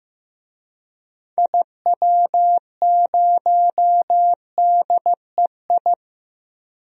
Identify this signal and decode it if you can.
Morse code, heard as a keyed tone, about 15 wpm, 690 Hz. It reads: IW0DEI
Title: DXCC-100 Call Signs - 6 Characters - 15wpm